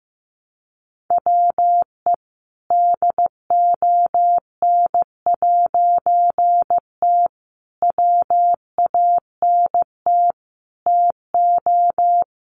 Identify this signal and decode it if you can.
Morse code, heard as a keyed tone, about 15 wpm, 700 Hz. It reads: WE DON'T WANT TO
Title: A Sound Of Thunder 15wpm